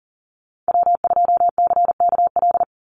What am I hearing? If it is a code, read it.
Morse code, heard as a keyed tone, about 40 wpm, 700 Hz. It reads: W2CKL
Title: NA SOTA Activator Callsigns 40wpm